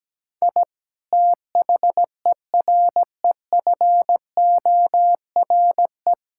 Morse code, heard as a keyed tone, about 17 wpm, 700 Hz. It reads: I THEREFORE
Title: Sentences from Top 1000 Words - Review Part 5 - 17wpm